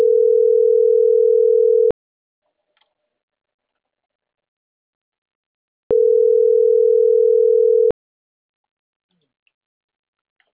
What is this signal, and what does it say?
Ringback tone (North America), 440 + 480 Hz.